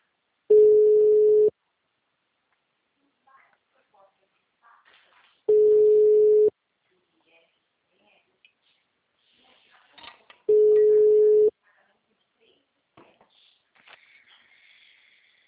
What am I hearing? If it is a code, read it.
Ringback tone (Europe, 425 Hz), with speech.